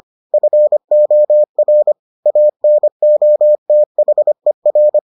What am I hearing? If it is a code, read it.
Morse code, heard as a keyed tone, about 25 wpm, 600 Hz. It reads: FOR ANOTHER